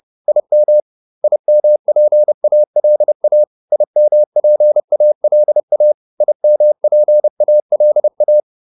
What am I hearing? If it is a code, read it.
Morse code, heard as a keyed tone, about 30 wpm, 600 Hz. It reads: IM IMPALA IMPALA IMPALA
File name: IM6 30